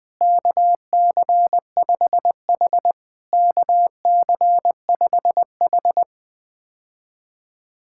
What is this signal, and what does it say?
Morse code, heard as a keyed tone, about 20 wpm, 700 Hz. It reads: KC5H KC5H
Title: POTA QSO Practice - 20wpm